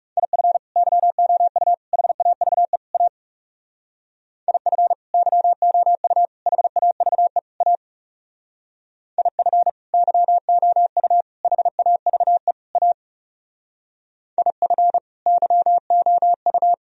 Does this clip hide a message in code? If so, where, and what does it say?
Yes: Morse code from 0.2 to 16.7 s, heard as a keyed tone, about 38 wpm, 700 Hz. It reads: IF YOU HAVE A IF YOU HAVE A IF YOU HAVE A IF YOU